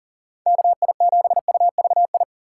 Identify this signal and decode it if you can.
Morse code, heard as a keyed tone, about 49 wpm, 700 Hz. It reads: KI7UVI
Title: Speed Racing - USA Call Signs 40wpm